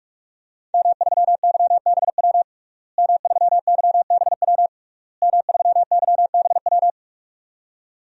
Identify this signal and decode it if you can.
Morse code, heard as a keyed tone, about 47 wpm, 700 Hz. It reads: M3YBW M3YBW M3YBW
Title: DXCC-100 Call Signs - 5 Characters - Repeated 3x - 45wpm